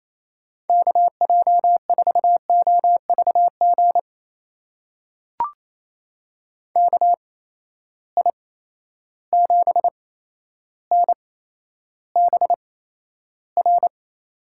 Morse code, heard as a keyed tone, about 28 wpm, spaced as for 12 wpm, 700 Hz. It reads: KJ4OVG KI7NBR
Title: USA Call Signs - 6 Characters - 8x Character Spacing - 28wpm